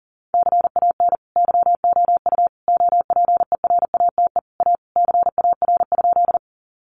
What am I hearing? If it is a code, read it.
Morse code, heard as a keyed tone, about 40 wpm, 700 Hz. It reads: CAN YOU OPERATE A CAR?
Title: Sentences from Top 1500 Words 40wpm